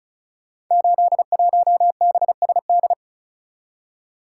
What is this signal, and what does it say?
Morse code, heard as a keyed tone, about 36 wpm, 700 Hz. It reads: 81BSD
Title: Code Groups - 5 Letter:Numbers - 35wpm